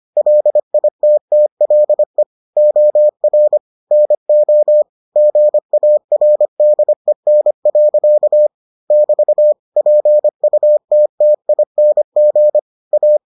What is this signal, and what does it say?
Morse code, heard as a keyed tone, about 25 wpm, 600 Hz. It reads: LITTLE OR NO GARDEN. <BT> PUTTING A